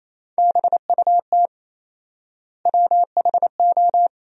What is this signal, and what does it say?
Morse code, heard as a keyed tone, about 28 wpm, 700 Hz. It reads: BUT WHO
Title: 4x Word-Spacing - Sentences from Top 100 Words - 28wpm